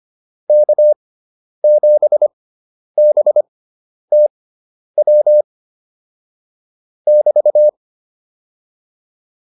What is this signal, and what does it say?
Morse code, heard as a keyed tone, about 25 wpm, spaced as for 10 wpm, 600 Hz. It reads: K7BTW <BT>